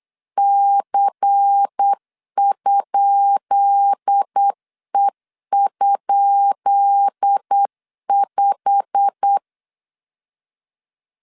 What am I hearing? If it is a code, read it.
Morse code, heard as a keyed tone, about 8 wpm, 800 Hz. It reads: C?E?5
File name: LEZIONI / 10 SIGNS / 10B INT 8wpm 3534